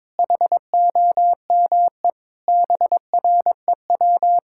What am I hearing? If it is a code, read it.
Morse code, heard as a keyed tone, about 22 wpm, 700 Hz. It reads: HOME BREW
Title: K1USN SST QSO Practice 22wpm